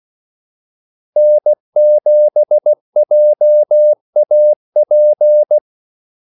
Morse code, heard as a keyed tone, about 16 wpm, 600 Hz. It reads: N7JAP
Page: Kaldesignaler 16 wpm